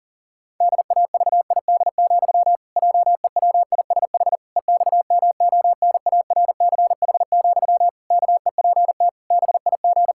Morse code, heard as a keyed tone, about 40 wpm, 700 Hz. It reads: DAVID, JEWISH EXMONARCH, KEPT BIG
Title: Pangram Challenge v2 - 40wpm